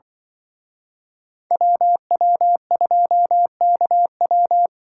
Morse code, heard as a keyed tone, about 24 wpm, 700 Hz. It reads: WW2KW
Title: Speed Racing - USA 2x2 Call Signs 20wpm